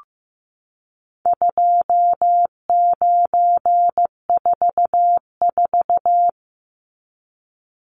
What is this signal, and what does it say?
Morse code, heard as a keyed tone, about 15 wpm, 700 Hz. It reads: 2944
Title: Numbers - 4-Digit 15wpm